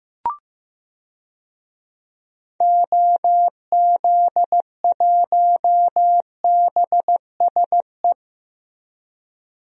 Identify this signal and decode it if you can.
Morse code, heard as a keyed tone, about 15 wpm, 700 Hz. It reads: OZ1BSE
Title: DXCC-100 Call Signs - 6 Characters - Repeated 3x - 15wpm